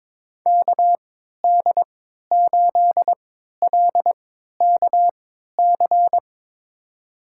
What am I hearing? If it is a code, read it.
Morse code, heard as a keyed tone, about 22 wpm, spaced as for 13 wpm, 700 Hz. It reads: KD8LKC
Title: USA Call Signs - 6 Characters - 3x Character Spacing - 22wpm